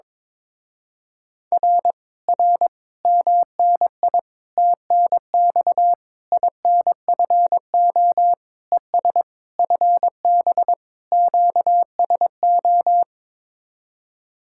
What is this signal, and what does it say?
Morse code, heard as a keyed tone, about 22 wpm, 700 Hz. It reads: R R MNI TNX INFO ES FB QSO
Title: Speed-Racing Common QSO Phrases - 22wpm